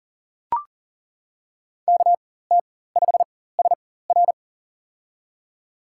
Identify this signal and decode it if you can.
Morse code, heard as a keyed tone, about 40 wpm, spaced as for 19 wpm, 700 Hz. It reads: KT5SR
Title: USA Call Signs - 5 Characters - 4x Character Spacing - 40wpm